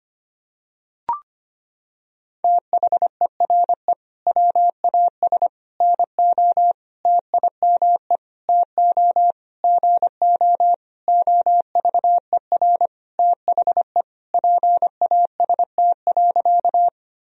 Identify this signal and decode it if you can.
Morse code, heard as a keyed tone, about 25 wpm, 700 Hz. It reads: THERE WAS NO TIME TO GO OVER THE PAST.
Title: Sentences from Top 1000 Words - Review Part 3 - 25wpm